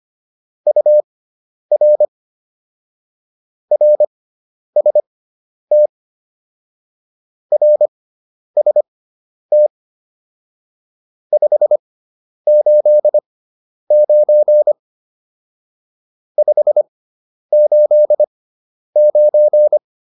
Morse code, heard as a keyed tone, about 25 wpm, spaced as for 10 wpm, 600 Hz. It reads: UR RST RST 589 589
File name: QSO409 10